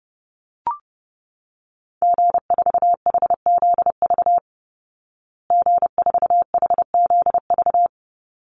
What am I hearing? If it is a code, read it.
Morse code, heard as a keyed tone, about 30 wpm, 700 Hz. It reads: G4HZV G4HZV
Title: Call Signs Repeated 3x - 30wpm